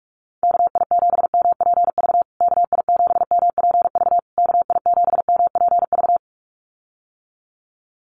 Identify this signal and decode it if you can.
Morse code, heard as a keyed tone, about 44 wpm, 700 Hz. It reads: KI7MPV KI7MPV KI7MPV
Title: USA Call Signs Repeated 3x 45wpm